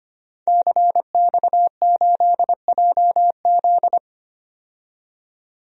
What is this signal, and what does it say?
Morse code, heard as a keyed tone, about 25 wpm, 700 Hz. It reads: CX8JZ
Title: DXCC-100 Call Signs - 5 Characters - 25wpm